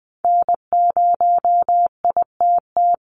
Morse code, heard as a keyed tone, about 20 wpm, 700 Hz. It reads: N0ITT
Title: USA Call Signs 20wpm